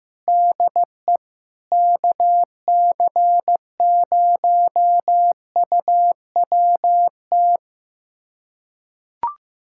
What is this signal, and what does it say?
Morse code, heard as a keyed tone, about 15 wpm, 700 Hz. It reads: DE KC0UWT
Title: USA Callsign DE Callsign - 6 Characters - 15wpm